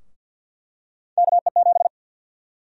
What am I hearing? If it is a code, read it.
Morse code, heard as a keyed tone, about 51 wpm, 700 Hz. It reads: KE7
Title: USA Call Sign Prefixes - Repeated 2x - 50wpm